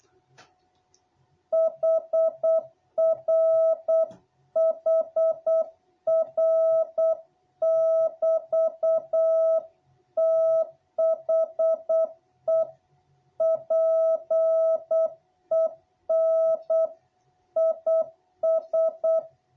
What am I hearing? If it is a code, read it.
Morse code, heard as a keyed tone, about 8 wpm, 660 Hz. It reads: HR HR<BT>THE PEN IS
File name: jh8cbh / 2021cw